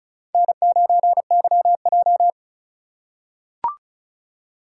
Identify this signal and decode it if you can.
Morse code, heard as a keyed tone, about 34 wpm, 700 Hz. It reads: N9YJ
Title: USA Call Signs - 4 Characters - 4x Character Spacing - 35wpm